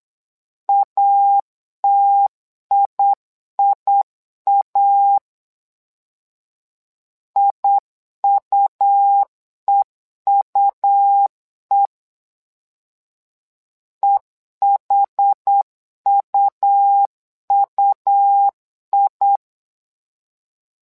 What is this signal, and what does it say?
Morse code, heard as a keyed tone, about 8 wpm, 800 Hz. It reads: ATIIA IUEUE EHUUI
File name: LEZIONI / 2 UHDN / 2 UH 8wpm 2129